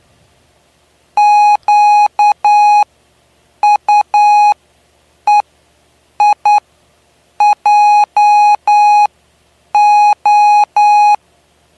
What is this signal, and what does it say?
Morse code, heard as a keyed tone, about 9 wpm, spaced as for 7 wpm, 830 Hz. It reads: QUEIJO
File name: cw / py2lcd / 09-PY2LCD-Li��o 9